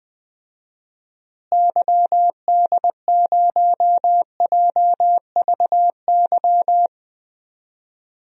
Morse code, heard as a keyed tone, about 20 wpm, 700 Hz. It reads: YD0JVY
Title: DXCC-100 Call Signs - 6 Characters - Repeated 2x - 20wpm